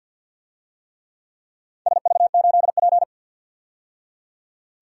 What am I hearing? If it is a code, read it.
Morse code, heard as a keyed tone, about 49 wpm, 700 Hz. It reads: SV8P
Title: DXCC-100 Call Signs - 4 Characters - 50wpm